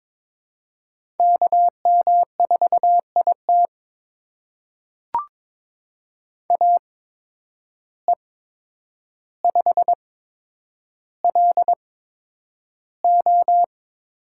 Morse code, heard as a keyed tone, about 22 wpm, spaced as for 9 wpm, 700 Hz. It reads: KM4IT AE5LO
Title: USA Call Signs - 5 Characters - 8x Character Spacing - 22wpm